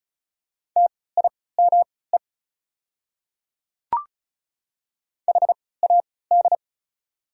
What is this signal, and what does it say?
Morse code, heard as a keyed tone, about 36 wpm, spaced as for 20 wpm, 690 Hz. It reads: TIME HAD
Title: 3x Character Spacing - Top 100 Words - 35wpm